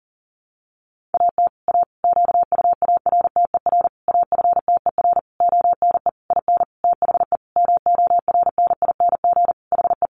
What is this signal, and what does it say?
Morse code, heard as a keyed tone, about 40 wpm, 700 Hz. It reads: AT A QUARTER AFTER ONE IN THE MORNING HE
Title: Sentences from Top 1300 Words 40wpm